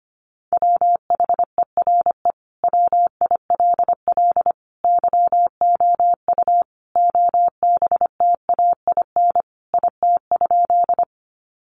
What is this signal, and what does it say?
Morse code, heard as a keyed tone, about 25 wpm, 700 Hz. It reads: WHERE WILL YOU OBTAIN IT?
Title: Sentences from Top 1300 Words 25wpm